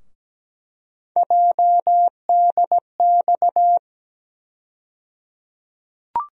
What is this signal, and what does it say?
Morse code, heard as a keyed tone, about 17 wpm, 700 Hz. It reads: JDX
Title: USA Call Sign Postfixes - 3 Letters 17wpm